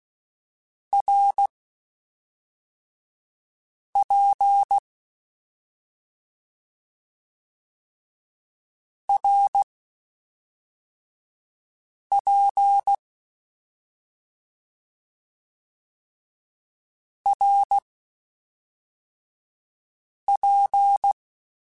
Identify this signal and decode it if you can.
Morse code, heard as a keyed tone, about 16 wpm, spaced as for 3 wpm, 800 Hz. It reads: RP RP RP
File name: Short Runs / Short Runs-mp3s / Balanced @5